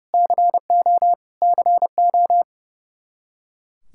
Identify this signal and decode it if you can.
Morse code, heard as a keyed tone, about 30 wpm, 700 Hz. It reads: CO CO
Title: Practice QSO - Exchange 1 - Beginning - 30wpm